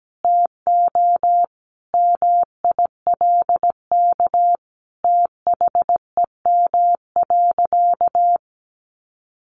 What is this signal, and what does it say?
Morse code, heard as a keyed tone, about 17 wpm, 700 Hz. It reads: TO MILK THEM.
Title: Sentences from Top 1500 Words 17wpm